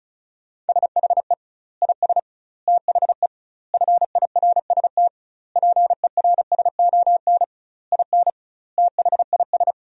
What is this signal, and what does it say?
Morse code, heard as a keyed tone, about 34 wpm, 700 Hz. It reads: SHE IS THE FIRST PERSON IN THIS
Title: 2x Word-Spacing - Sentences from Top 400 Words - 35wpm